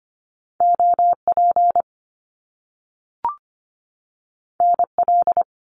Morse code, heard as a keyed tone, about 25 wpm, 700 Hz. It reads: OP NL
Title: Top Two Letter Combinations in Words 25wpm